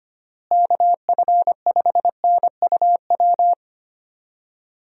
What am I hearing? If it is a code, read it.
Morse code, heard as a keyed tone, about 25 wpm, 700 Hz. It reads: KF5NUW
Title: USA Call Signs - 6 Characters - Repeated 2x - 25wpm